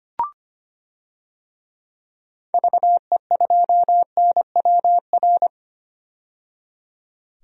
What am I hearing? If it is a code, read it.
Morse code, heard as a keyed tone, about 25 wpm, 700 Hz. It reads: VE2NWR
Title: DXCC-100 Call Signs - 6 Characters - 25wpm